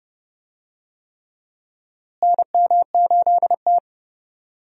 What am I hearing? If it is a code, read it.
Morse code, heard as a keyed tone, about 30 wpm, 700 Hz. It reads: NM8T
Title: USA Call Signs - 4 Characters - Repeated 3x - 30wpm